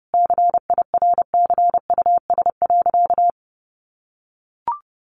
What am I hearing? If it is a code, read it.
Morse code, heard as a keyed tone, about 30 wpm, 700 Hz. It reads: CIRCUS.
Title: Easy Phrases 30wpm